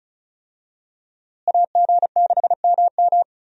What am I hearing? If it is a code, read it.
Morse code, heard as a keyed tone, about 36 wpm, 700 Hz. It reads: AG6MM
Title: USA Call Signs - 5 Characters - Repeated 3x - 35wpm